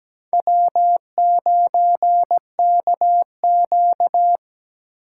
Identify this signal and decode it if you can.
Morse code, heard as a keyed tone, about 17 wpm, 700 Hz. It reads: W9KQ
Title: USA Call Signs - 4 Characters - 8x Character Spacing - 17wpm